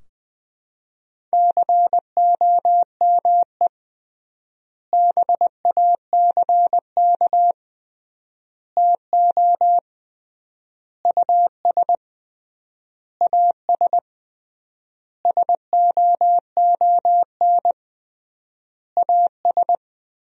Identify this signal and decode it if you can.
Morse code, heard as a keyed tone, about 20 wpm, 700 Hz. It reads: COME BACK TO US AS SOON AS
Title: 3x Word-Spacing - Sentences from Top 500 Words - 20wpm